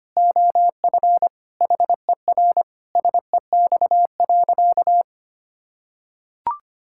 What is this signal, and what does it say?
Morse code, heard as a keyed tone, about 25 wpm, 700 Hz. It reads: OF HER SEX.
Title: Sentences from Top 1600 Words 25wpm